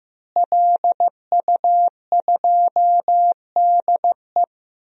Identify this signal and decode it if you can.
Morse code, heard as a keyed tone, about 15 wpm, 700 Hz. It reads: LU2DE
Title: DXCC-100 Call Signs - 5 Characters - Repeated 3x - 15wpm